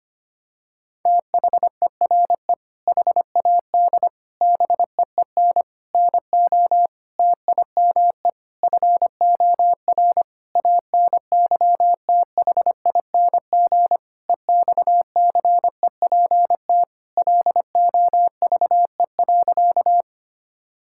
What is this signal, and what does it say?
Morse code, heard as a keyed tone, about 25 wpm, 700 Hz. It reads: THERE HAD BEEN NO TIME FOR ANYTHING EXCEPT LOVE.
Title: Sentences from Top 1600 Words 25wpm